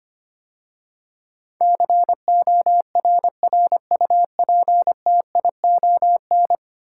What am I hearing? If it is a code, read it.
Morse code, heard as a keyed tone, about 25 wpm, 700 Hz. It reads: CORRUPTION
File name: 10 Letter Words - First 3 Letters Given - 25wpm